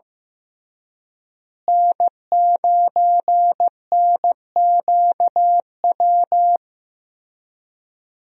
Morse code, heard as a keyed tone, about 16 wpm, 700 Hz. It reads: N9NQW
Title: USA Call Signs - 5 Characters - Repeated 3x - 15wpm